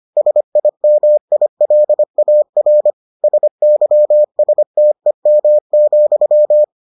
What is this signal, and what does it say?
Morse code, heard as a keyed tone, about 25 wpm, 600 Hz. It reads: SIMILAR SYSTEM,